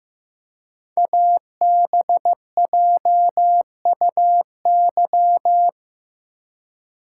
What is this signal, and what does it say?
Morse code, heard as a keyed tone, about 15 wpm, 700 Hz. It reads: ABJUY
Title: Code Groups - 5 Letters - 15wpm